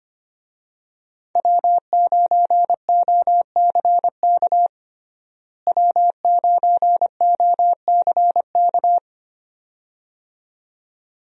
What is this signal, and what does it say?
Morse code, heard as a keyed tone, about 25 wpm, 700 Hz. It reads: W9OCK W9OCK
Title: USA Call Signs - 5 Characters - Repeated 2x - 25wpm